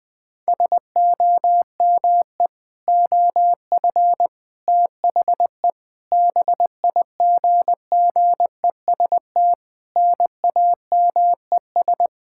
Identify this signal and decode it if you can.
Morse code, heard as a keyed tone, about 21 wpm, 700 Hz. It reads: SOME OF THE BIGGEST NAMES
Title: Sentences with 7-Word Phrases - 20wpm